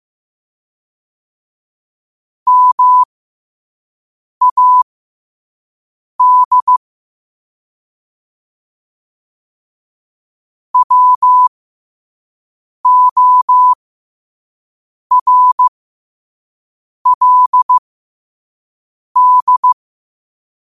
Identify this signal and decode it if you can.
Morse code, heard as a keyed tone, about 15 wpm, spaced as for 5 wpm, 1000 Hz. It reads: MAD WORLD
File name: qotdfiles / 5wpm